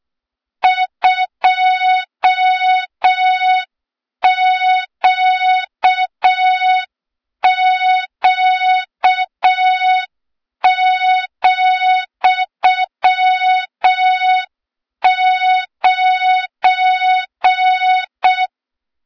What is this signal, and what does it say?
Morse code, heard as a keyed tone, about 6 wpm, 3700 Hz. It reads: 2QQ,9
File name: Morze-HA2MN / hang / 23 GYAKORLAT